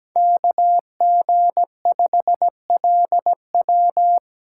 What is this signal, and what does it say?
Morse code, heard as a keyed tone, about 17 wpm, 700 Hz. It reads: KG5LW
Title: USA Call Signs - 5 Characters - 2x Character Spacing - 17wpm